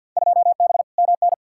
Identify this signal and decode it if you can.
Morse code, heard as a keyed tone, about 50 wpm, 700 Hz. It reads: 2B MN
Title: Field Day Exchanges - 50wpm